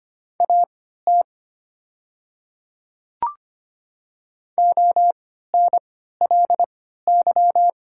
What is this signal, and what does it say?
Morse code, heard as a keyed tone, about 25 wpm, spaced as for 14 wpm, 690 Hz. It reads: AT ONLY
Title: 3x Character Spacing - Top 200 Words - 25wpm